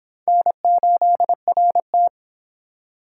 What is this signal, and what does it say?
Morse code, heard as a keyed tone, about 25 wpm, 700 Hz. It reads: N8RT